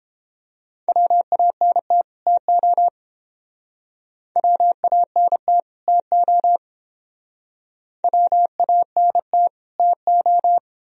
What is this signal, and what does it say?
Morse code, heard as a keyed tone, about 29 wpm, 700 Hz. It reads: WANT TO WANT TO WANT TO